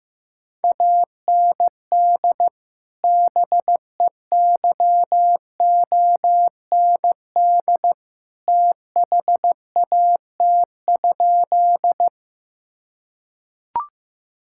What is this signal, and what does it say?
Morse code, heard as a keyed tone, about 15 wpm, 700 Hz. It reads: AND BEYOND THAT?
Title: Sentences from Top 1000 Words - Review Part 6 - 15wpm